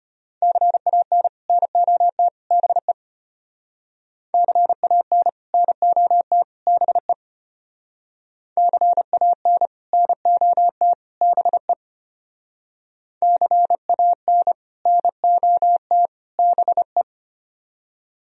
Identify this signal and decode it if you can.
Morse code, heard as a keyed tone, about 33 wpm, 700 Hz. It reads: CAN NOT BE CAN NOT BE CAN NOT BE CAN NOT BE